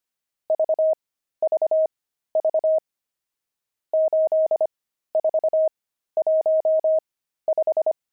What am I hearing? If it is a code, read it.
Morse code, heard as a keyed tone, about 25 wpm, spaced as for 13 wpm, 630 Hz. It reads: VVV 8415
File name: pangram-fs-13-cs-25-nums4